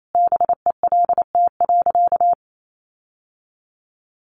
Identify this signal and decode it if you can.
Morse code, heard as a keyed tone, about 28 wpm, 700 Hz. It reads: BELT.
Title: Sentences from Top 1400 Words 28wpm